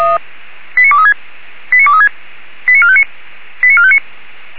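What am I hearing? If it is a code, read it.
Five-tone selective call (ZVEI), heard as quick runs of five tones, tones of about 80 ms. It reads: 89236 89236 89459 89459
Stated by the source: Five-tone ZVEI selective call received by SDR.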